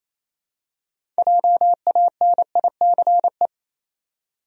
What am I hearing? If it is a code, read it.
Morse code, heard as a keyed tone, about 28 wpm, 700 Hz. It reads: JANICE
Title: USA Common Names - 28wpm